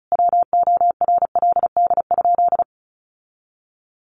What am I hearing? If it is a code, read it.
Morse code, heard as a keyed tone, about 34 wpm, 700 Hz. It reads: WORLD?
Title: Sentences from Top 1400 Words 35wpm